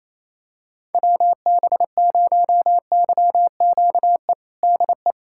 Morse code, heard as a keyed tone, about 28 wpm, 700 Hz. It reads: WB0YQE DE
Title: USA Callsign DE Callsign - 6 Characters - 28wpm